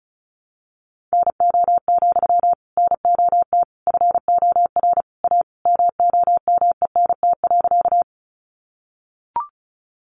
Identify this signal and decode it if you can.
Morse code, heard as a keyed tone, about 34 wpm, 700 Hz. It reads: NO, NOT FOR A MOMENT.
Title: Sentences from Top 1000 Words 35wpm